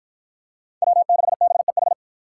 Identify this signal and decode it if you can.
Morse code, heard as a keyed tone, about 48 wpm, 700 Hz. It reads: W6BEH